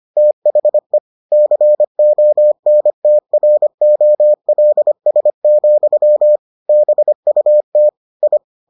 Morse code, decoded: THE CONTROLS, BUT I

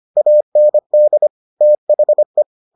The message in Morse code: AND THE

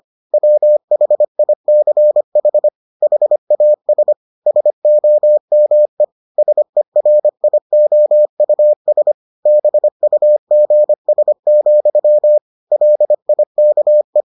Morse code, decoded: WHICH HAS SOME SERIOUS BUGS, LIKE

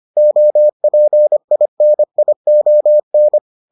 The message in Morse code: OPINION

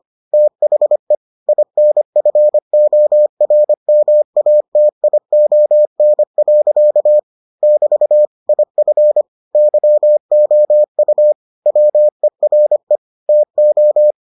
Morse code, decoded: THE INFORMATION. <BT> IF YOU WERE TO

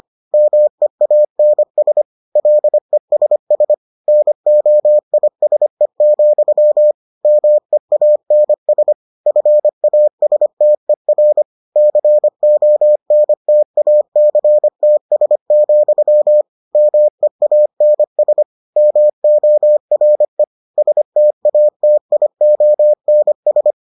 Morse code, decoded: MEANS LESS NOISE, MEANS FASTER CONTACTS, MEANS MORE STATIONS

